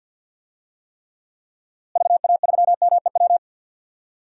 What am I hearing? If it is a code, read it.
Morse code, heard as a keyed tone, about 52 wpm, 700 Hz. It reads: VA3MEW
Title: DXCC-100 Call Signs - 6 Characters - Repeated 3x - 50wpm